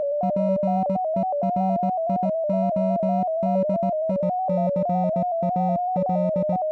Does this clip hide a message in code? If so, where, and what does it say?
Yes: Morse code from 0.2 to 6.6 s, heard as a keyed tone, about 18 wpm, 200 Hz. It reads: PERIODICAL